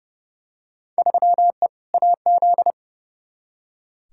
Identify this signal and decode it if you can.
Morse code, heard as a keyed tone, about 31 wpm, 700 Hz. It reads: 3E AZ
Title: Field Day Exchanges Encore - 30wpm